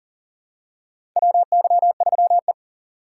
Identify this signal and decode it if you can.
Morse code, heard as a keyed tone, about 40 wpm, 700 Hz. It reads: WY3E
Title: USA Call Signs - 4 Characters - 2x Character Spacing - 40wpm